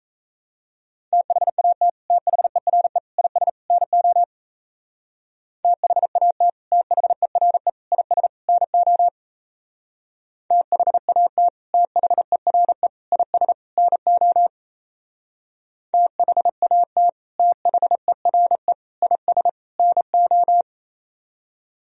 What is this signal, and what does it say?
Morse code, heard as a keyed tone, about 35 wpm, 700 Hz. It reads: THAT THERE IS NO THAT THERE IS NO THAT THERE IS NO THAT THERE IS NO
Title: Speed-Racing - Sets of 4 Words 28wpm